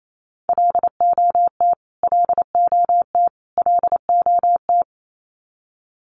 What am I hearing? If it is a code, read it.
Morse code, heard as a keyed tone, about 28 wpm, 700 Hz. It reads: LOT LOT LOT
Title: Top 500 Words Repeated 28wpm